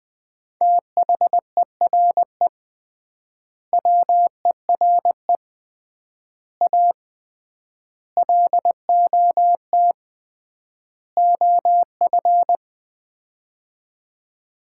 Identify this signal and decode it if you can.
Morse code, heard as a keyed tone, about 21 wpm, 700 Hz. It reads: THERE WERE A LOT OF
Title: 3x Word-Spacing - Sets of 5 Words - 20wpm